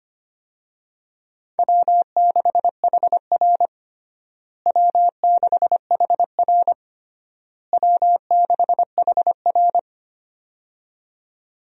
Morse code, heard as a keyed tone, about 25 wpm, 700 Hz. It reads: W6HR W6HR W6HR
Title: USA Call Signs - 4 Characters - Repeated 3x - 25wpm